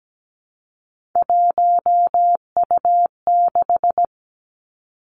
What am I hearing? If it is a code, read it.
Morse code, heard as a keyed tone, about 17 wpm, 700 Hz. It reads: 1U6
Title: Three Challenging Letter-Number Combinations 17wpm